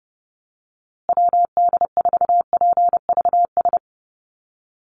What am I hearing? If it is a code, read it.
Morse code, heard as a keyed tone, about 30 wpm, 700 Hz. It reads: WD4PVS
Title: USA Call Signs Repeated 2x 30wpm